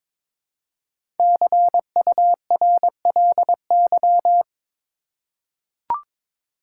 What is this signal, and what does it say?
Morse code, heard as a keyed tone, about 22 wpm, 700 Hz. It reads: CURLY
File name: 5 Letter Words - First 3 Letters Given - 22wpm